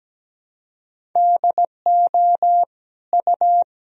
Morse code, heard as a keyed tone, about 17 wpm, 700 Hz. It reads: DO U
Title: Sentences from Top 200 Words and Abbreviations 17wpm